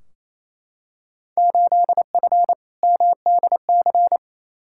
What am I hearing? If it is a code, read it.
Morse code, heard as a keyed tone, about 28 wpm, 700 Hz. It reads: 8F MDC
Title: Field Day Exchanges Encore - 28wpm